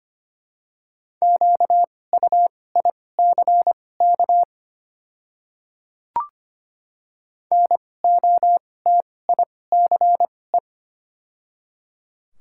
Morse code, heard as a keyed tone, about 25 wpm, spaced as for 18 wpm, 700 Hz. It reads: QUICK NOTICE
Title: 2x Character Spacing - Top 500 Words - 25wpm